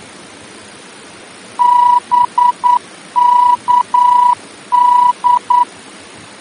Morse code, heard as a keyed tone, about 9 wpm, 980 Hz. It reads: BKD
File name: germany / mx / BKD-D-VOR